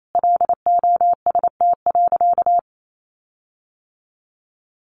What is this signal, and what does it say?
Morse code, heard as a keyed tone, about 28 wpm, 700 Hz. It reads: LOST.
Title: Sentences from Top 1400 Words 28wpm